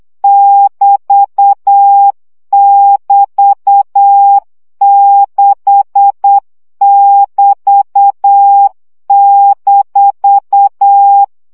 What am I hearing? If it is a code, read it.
Morse code, heard as a keyed tone, about 8 wpm, 800 Hz. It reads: <BT><BT>6<BT>-
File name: CW gyakorlas b 6 egyenlo minusz 7 baud